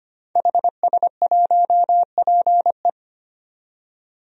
Morse code, heard as a keyed tone, about 25 wpm, 700 Hz. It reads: HS1PE
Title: DXCC-100 Call Signs - 5 Characters - 25wpm